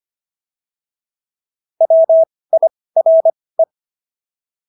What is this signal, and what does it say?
Morse code, heard as a keyed tone, about 25 wpm, spaced as for 18 wpm, 650 Hz. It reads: WIRE